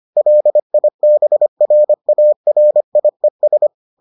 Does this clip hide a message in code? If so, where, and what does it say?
Yes: Morse code from 0.2 to 3.7 s, heard as a keyed tone, about 25 wpm, 600 Hz. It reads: LIBRARIES